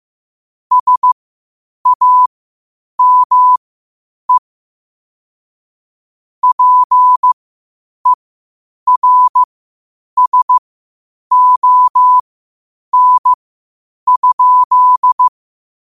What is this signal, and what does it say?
Morse code, heard as a keyed tone, about 15 wpm, spaced as for 8 wpm, 1000 Hz. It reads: SAME PERSON?
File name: qotdfiles / 8wpm